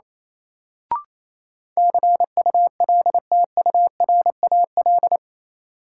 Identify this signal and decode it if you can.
Morse code, heard as a keyed tone, about 28 wpm, 700 Hz. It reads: CULTURAL
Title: Sentences from Top 1500 Words - New Word Drill Alt - 28wpm